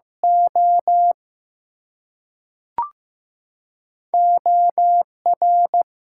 Morse code, heard as a keyed tone, about 15 wpm, 700 Hz. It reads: O OR
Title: Two Letter Words - 15wpm